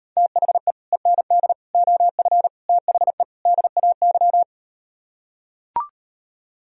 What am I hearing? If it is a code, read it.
Morse code, heard as a keyed tone, about 38 wpm, 700 Hz. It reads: THE END OF THE DAY